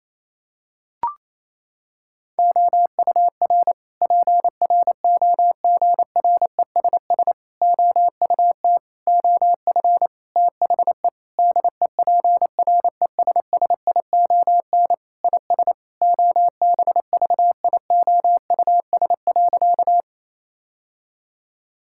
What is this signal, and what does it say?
Morse code, heard as a keyed tone, about 28 wpm, 700 Hz. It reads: OUR PROGRESS OUT OF THE DEPRESSION IS OBVIOUS.
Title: Sentences from Top 1600 Words 28wpm